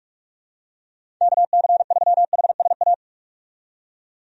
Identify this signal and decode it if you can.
Morse code, heard as a keyed tone, about 41 wpm, 700 Hz. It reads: KC3HSA
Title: USA Call Signs - 6 Characters - 45wpm